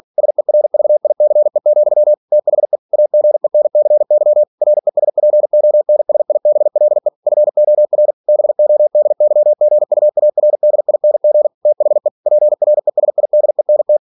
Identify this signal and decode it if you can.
Morse code, heard as a keyed tone, about 46 wpm, 600 Hz. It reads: SERVICE, THE AGENCY RESPONSIBLE FOR BODYGUARDING THE PRESIDENT
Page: Never 47wpm